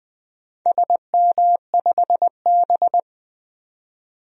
Morse code, heard as a keyed tone, about 20 wpm, 700 Hz. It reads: SM5B
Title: DXCC-100 Call Signs - 4 Characters - 20wpm